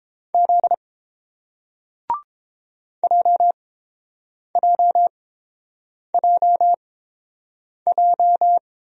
Morse code, heard as a keyed tone, about 30 wpm, spaced as for 7 wpm, 700 Hz. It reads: Z JJJJ